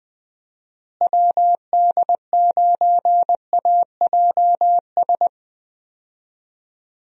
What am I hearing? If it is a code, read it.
Morse code, heard as a keyed tone, about 20 wpm, 700 Hz. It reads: WD9AJS
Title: USA Call Signs - 6 Characters - Repeated 3x - 20wpm